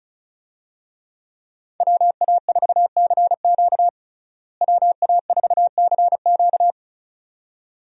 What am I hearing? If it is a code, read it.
Morse code, heard as a keyed tone, about 34 wpm, 700 Hz. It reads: WA4CQ WA4CQ
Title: USA Call Signs - 5 Characters - Repeated 2x - 35wpm